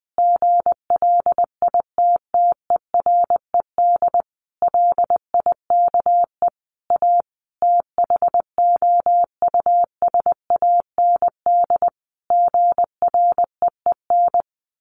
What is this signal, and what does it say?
Morse code, heard as a keyed tone, about 19 wpm, 700 Hz. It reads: GLITTERED LIKE A THOUSAND GREEN
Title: A Sound Of Thunder 20wpm